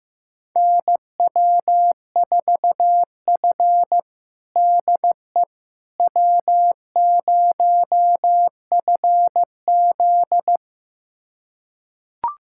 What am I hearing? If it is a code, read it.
Morse code, heard as a keyed tone, about 16 wpm, 700 Hz. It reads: NW4F DE W0FZ
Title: USA Callsign DE Callsign - 4 Characters - 15wpm